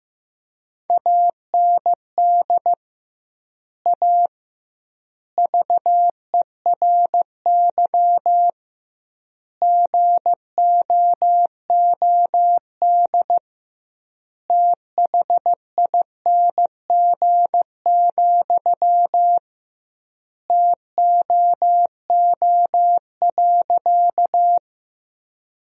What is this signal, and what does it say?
Morse code, heard as a keyed tone, about 16 wpm, 700 Hz. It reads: AND A VERY GOOD THING, TOO.
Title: 2x Word-Spacing - Sentences from Top 200 Words - 15wpm